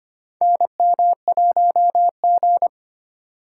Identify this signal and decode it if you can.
Morse code, heard as a keyed tone, about 25 wpm, 700 Hz. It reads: NM1G
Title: USA Call Signs - 4 Characters - 3x Character Spacing - 25wpm